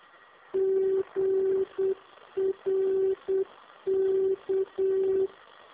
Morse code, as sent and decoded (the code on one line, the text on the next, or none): --. .-. -.-
GRK